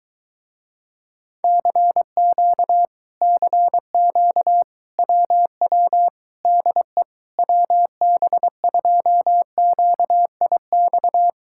Morse code, heard as a keyed tone, about 23 wpm, 700 Hz. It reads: CQ CQ WW DE WB2QIX